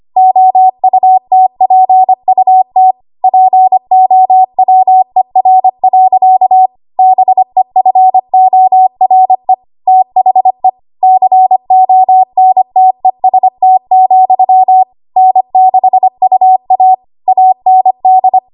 Morse code, heard as a keyed tone, about 25 wpm, 750 Hz. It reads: OUTPUT POWER. BEFORE THE CONTEST, N6UA AND
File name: Morse / 25WPM